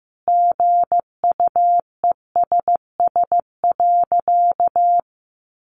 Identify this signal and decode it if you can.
Morse code, heard as a keyed tone, about 15 wpm, 700 Hz. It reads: GUESS.
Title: Sentences from Top 1500 Words 15wpm